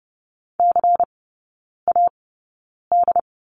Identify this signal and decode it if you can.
Morse code, heard as a keyed tone, about 30 wpm, spaced as for 9 wpm, 700 Hz. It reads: CAD
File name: Lesson 16e - Letter c - Three Characters 30wpm